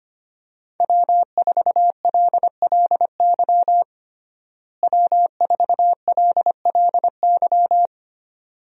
Morse code, heard as a keyed tone, about 25 wpm, 700 Hz. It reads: W4LLY W4LLY
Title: USA Call Signs - 5 Characters - Repeated 2x - 25wpm